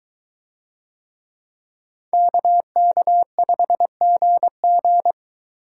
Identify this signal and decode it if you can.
Morse code, heard as a keyed tone, about 23 wpm, 700 Hz. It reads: KK5GG